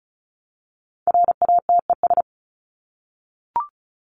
Morse code, heard as a keyed tone, about 34 wpm, 700 Hz. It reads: RATES
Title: ES:S Suffix - Top 100 Words with Base Words 35wpm